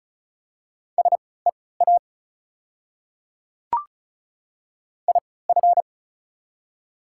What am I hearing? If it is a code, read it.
Morse code, heard as a keyed tone, about 34 wpm, spaced as for 20 wpm, 700 Hz. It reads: SEA IF
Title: 3x Character Spacing - Top 300 Words - 35wpm